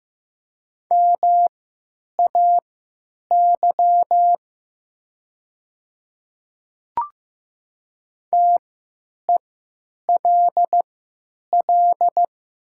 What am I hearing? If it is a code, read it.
Morse code, heard as a keyed tone, about 15 wpm, spaced as for 9 wpm, 700 Hz. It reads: MAY TELL
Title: 3x Character Spacing - Top 200 Words - 15wpm